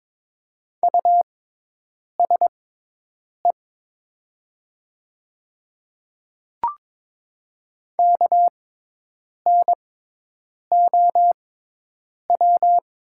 Morse code, heard as a keyed tone, about 22 wpm, spaced as for 8 wpm, 700 Hz. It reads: USE KNOW
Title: 6x Character Spacing - Top 100 Words - 22wpm